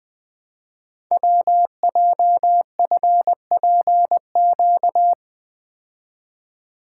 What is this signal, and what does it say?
Morse code, heard as a keyed tone, about 20 wpm, 700 Hz. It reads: WJFPQ
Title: Code Groups - 5 Letters - 20wpm